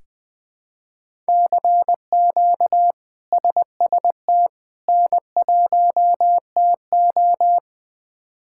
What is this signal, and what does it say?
Morse code, heard as a keyed tone, about 20 wpm, 700 Hz. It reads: CQ SST N1TO
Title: K1USN SST QSO Practice - 20wpm